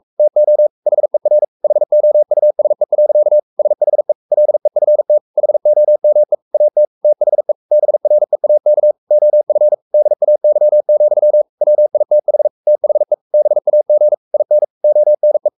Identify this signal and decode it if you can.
Morse code, heard as a keyed tone, about 44 wpm, 600 Hz. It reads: TO HER HOUSE. SHE LEFT HOME AT THE BREAK OF DAY, WITH THE BAG IN ONE
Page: Romanen Never af Ken Follet på engelsk.